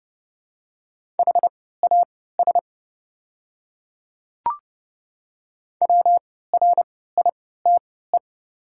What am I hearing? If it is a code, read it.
Morse code, heard as a keyed tone, about 29 wpm, spaced as for 17 wpm, 700 Hz. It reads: HAS WRITE
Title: 3x Character Spacing - Top 500 Words - 30wpm